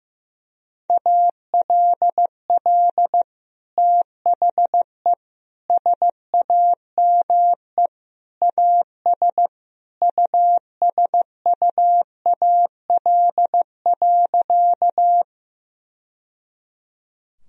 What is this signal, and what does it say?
Morse code, heard as a keyed tone, about 15 wpm, 700 Hz. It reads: ALL THE SAME AS USUAL.
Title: Sentences from Top 1000 Words - Review Part 2 - 15wpm